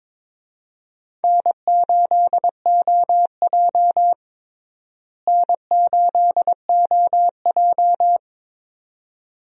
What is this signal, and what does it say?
Morse code, heard as a keyed tone, about 22 wpm, 700 Hz. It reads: N8OJ N8OJ
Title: USA Call Signs - 4 Characters - Repeated 2x - 22wpm